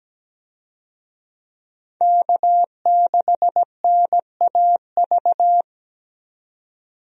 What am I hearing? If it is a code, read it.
Morse code, heard as a keyed tone, about 17 wpm, 700 Hz. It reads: K6NAV
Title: USA Call Signs - 5 Characters - Repeated 3x - 17wpm